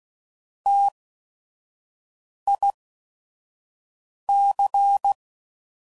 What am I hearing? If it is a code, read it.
Morse code, heard as a keyed tone, about 16 wpm, spaced as for 5 wpm, 800 Hz. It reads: TIC